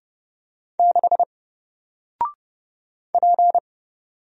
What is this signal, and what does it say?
Morse code, heard as a keyed tone, about 30 wpm, 700 Hz. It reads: 6 P
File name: Single-Letter-Number-30wpm